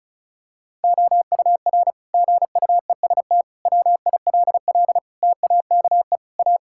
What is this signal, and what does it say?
Morse code, heard as a keyed tone, about 34 wpm, 700 Hz. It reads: OUR GUEST WILL TAKE A
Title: Sentences from Top 1600 Words 35wpm